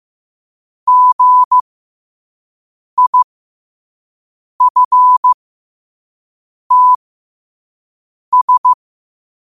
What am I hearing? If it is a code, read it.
Morse code, heard as a keyed tone, about 15 wpm, spaced as for 5 wpm, 1000 Hz. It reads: GIFTS